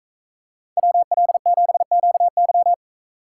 Morse code, heard as a keyed tone, about 42 wpm, 700 Hz. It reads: WL7QY